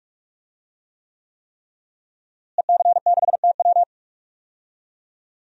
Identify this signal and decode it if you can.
Morse code, heard as a keyed tone, about 46 wpm, 700 Hz. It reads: EC6TW
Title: DXCC-100 Call Signs - 5 Characters - Repeated 3x - 45wpm